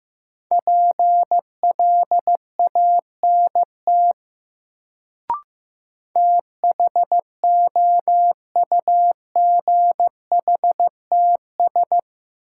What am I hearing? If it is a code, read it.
Morse code, heard as a keyed tone, about 15 wpm, 700 Hz. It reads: PLANT THOUGHTS
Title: Sentences from Top 1000 Words - Review Part 1 - New Word Drill - 15wpm